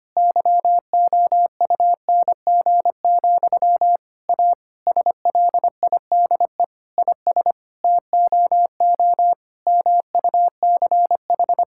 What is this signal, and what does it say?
Morse code, heard as a keyed tone, about 25 wpm, 700 Hz. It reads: YOUNG, A SLIDE IS TOO MUCH
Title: Sentences from Top 1600 Words 25wpm